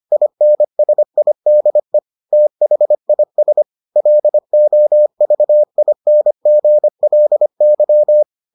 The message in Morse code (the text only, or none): INSIDE THIS LOVINGLY